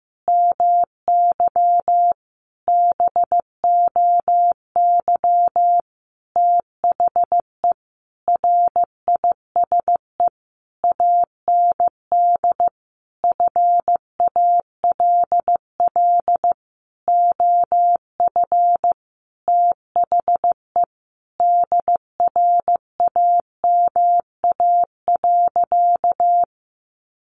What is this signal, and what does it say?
Morse code, heard as a keyed tone, about 15 wpm, 700 Hz. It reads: MY BOY THE RISE AND FALL OF THE DRAMA.
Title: Sentences from Top 1500 Words 15wpm